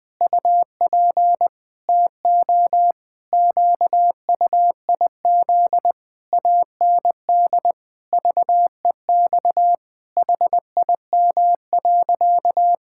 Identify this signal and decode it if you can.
Morse code, heard as a keyed tone, about 20 wpm, 700 Hz. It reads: UP TO QUIZ AND VEX HIM.
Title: Pangram Challenge v2 - 20wpm